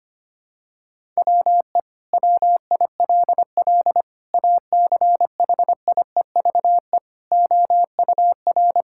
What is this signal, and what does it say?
Morse code, heard as a keyed tone, about 25 wpm, 700 Hz. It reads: WE WILL ACHIEVE OUR
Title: Sentences from Top 1000 Words - Review Part 6 - 25wpm